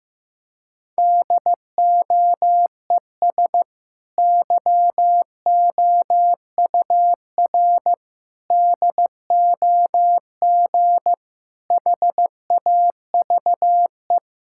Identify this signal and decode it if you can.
Morse code, heard as a keyed tone, about 15 wpm, 700 Hz. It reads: DOES YOUR DOG HAVE
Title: Sentences from Top 1600 Words 15wpm